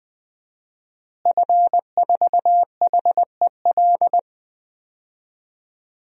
Morse code, heard as a keyed tone, about 20 wpm, 700 Hz. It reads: F4HEL
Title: DXCC-100 Call Signs - 5 Characters - Repeated 3x - 20wpm